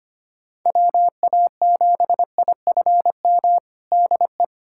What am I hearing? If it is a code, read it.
Morse code, heard as a keyed tone, about 25 wpm, 700 Hz. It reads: WA7IFM DE
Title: USA Callsign DE Callsign - 6 Characters - 25wpm